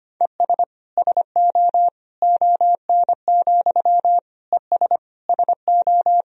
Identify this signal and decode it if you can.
Morse code, heard as a keyed tone, about 25 wpm, 700 Hz. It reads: ES SO ON, ES SO
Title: Sentences from Top 100 Words and Abbreviations 25wpm